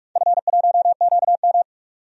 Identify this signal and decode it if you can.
Morse code, heard as a keyed tone, about 43 wpm, 700 Hz. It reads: F1QM
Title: DXCC-100 Call Signs - 4 Characters - 45wpm